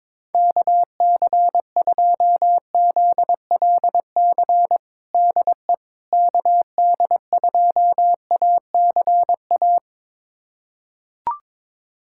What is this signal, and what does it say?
Morse code, heard as a keyed tone, about 22 wpm, 700 Hz. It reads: KC2ZLC DE KD2ACA
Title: USA Callsign DE Callsign 22wpm